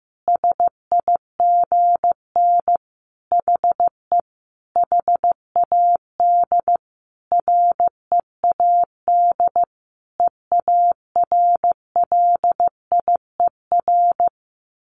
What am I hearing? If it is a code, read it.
Morse code, heard as a keyed tone, about 15 wpm, 700 Hz. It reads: SIGN HE HAD READ EARLIER
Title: A Sound Of Thunder 15wpm